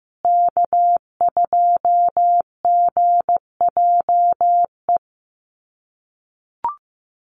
Morse code, heard as a keyed tone, about 15 wpm, 700 Hz. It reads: K2GJE
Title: USA Call Signs Repeated 2x 15wpm